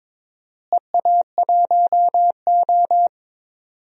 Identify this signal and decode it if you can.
Morse code, heard as a keyed tone, about 22 wpm, 700 Hz. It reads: EA1O
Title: DXCC-100 Call Signs - 4 Characters - Repeated 2x - 22wpm